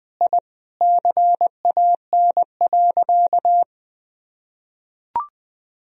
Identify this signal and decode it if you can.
Morse code, heard as a keyed tone, about 20 wpm, 700 Hz. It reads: I CAN.
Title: Sentences from Top 1000 Words - Review Part 3 - 20wpm